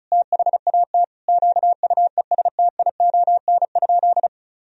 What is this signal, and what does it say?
Morse code, heard as a keyed tone, about 36 wpm, 700 Hz. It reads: THAT QUESTION?
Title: Sentences from Top 400 Words and Abbreviations 35wpm